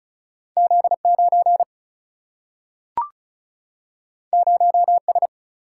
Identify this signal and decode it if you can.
Morse code, heard as a keyed tone, about 36 wpm, 700 Hz. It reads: Z9 0S
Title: Single-Letter Single-Number 35wpm